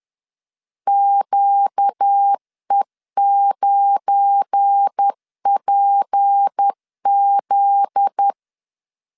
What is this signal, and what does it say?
Morse code, heard as a keyed tone, about 10 wpm, 800 Hz. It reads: QE9PZ
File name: LEZIONI / 8 YJ45 / 8 YJ45 10wpm 2549